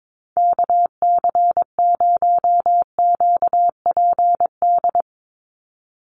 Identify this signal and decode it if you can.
Morse code, heard as a keyed tone, about 22 wpm, 700 Hz. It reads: KC0QPD
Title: USA Call Signs 22wpm